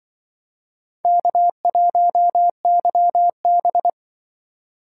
Morse code, heard as a keyed tone, about 24 wpm, 700 Hz. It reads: K1YB
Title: Speed Racing - USA Call Signs 20wpm